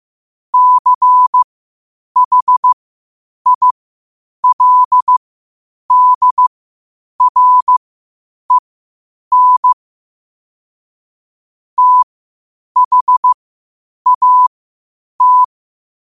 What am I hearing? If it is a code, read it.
Morse code, heard as a keyed tone, about 15 wpm, spaced as for 8 wpm, 1000 Hz. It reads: CHILDREN THAT
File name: qotdfiles / 8wpm